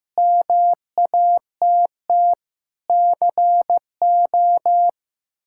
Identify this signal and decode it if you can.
Morse code, heard as a keyed tone, about 15 wpm, 700 Hz. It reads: MATT CO
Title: K1USN SST QSO Practice - 15wpm